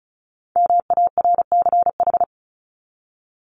Morse code, heard as a keyed tone, about 34 wpm, 700 Hz. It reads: MARCH
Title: Top 1501-1750 Words 35wpm